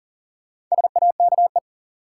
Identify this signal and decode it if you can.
Morse code, heard as a keyed tone, about 40 wpm, 700 Hz. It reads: SAKE
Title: Sets of 4 Words - New Word Drill - 40wpm